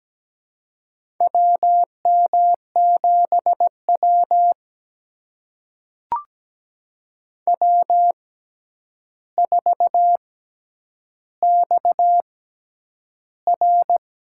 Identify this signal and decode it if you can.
Morse code, heard as a keyed tone, about 17 wpm, spaced as for 9 wpm, 700 Hz. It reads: WM7W W4XR
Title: USA Call Signs - 4 Characters - 6x Character Spacing - 17wpm